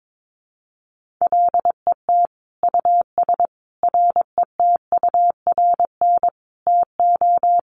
Morse code, heard as a keyed tone, about 22 wpm, 700 Hz. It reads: LET US RETURN TO
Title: Sentences from Top 1500 Words 22wpm